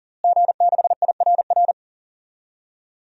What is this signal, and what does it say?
Morse code, heard as a keyed tone, about 40 wpm, 700 Hz. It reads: G6IRR
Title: DXCC-100 Call Signs - 5 Characters - Repeated 3x - 40wpm